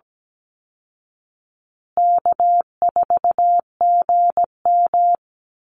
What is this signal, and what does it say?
Morse code, heard as a keyed tone, about 17 wpm, 700 Hz. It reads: K4GM
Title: Easy Call Signs Repeated 3x - 17wpm